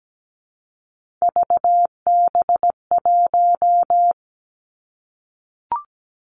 Morse code, heard as a keyed tone, about 17 wpm, 700 Hz. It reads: VB1
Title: Three Challenging Letter-Number Combinations 17wpm